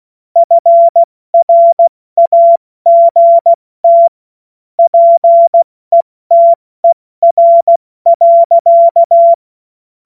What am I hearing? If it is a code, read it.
Morse code, heard as a keyed tone, about 16 wpm, 680 Hz. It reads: FRAGT PETER.